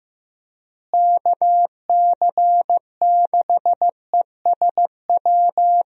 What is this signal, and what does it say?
Morse code, heard as a keyed tone, about 15 wpm, 700 Hz. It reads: KC6ESW
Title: USA Call Signs - 6 Characters - Repeated 2x - 15wpm